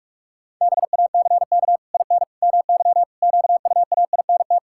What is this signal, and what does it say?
Morse code, heard as a keyed tone, about 46 wpm, 700 Hz. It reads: BACK IN MY QUAINT
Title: Pangram Challenge v2 - 45wpm